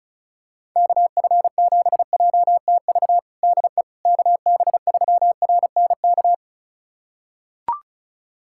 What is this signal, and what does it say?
Morse code, heard as a keyed tone, about 36 wpm, 700 Hz. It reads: KF7JTV DE KB3RNK
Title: USA Callsign DE Callsign 35wpm